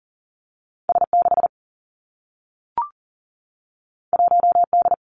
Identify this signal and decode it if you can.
Morse code, heard as a keyed tone, about 40 wpm, 700 Hz. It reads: S 6 1 D
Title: Two Challenging Letter-Number Combinations 40wpm